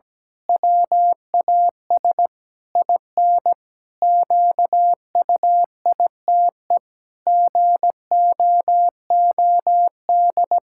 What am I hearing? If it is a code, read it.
Morse code, heard as a keyed tone, about 17 wpm, 700 Hz. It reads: WAS IN QUITE GOOD
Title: Pangram Challenge v2 - 17wpm